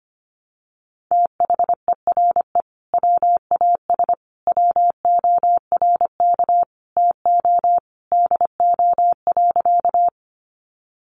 Morse code, heard as a keyed tone, about 25 wpm, 700 Hz. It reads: THERE WAS WORK TO DO.
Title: Sentences from Top 200 Words 25wpm